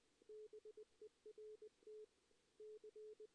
Morse code, heard as a keyed tone, about 21 wpm, 440 Hz. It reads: BERT C
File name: 2016 JAN CW NAQP / 2016 Jan naqp k5ot - 0-3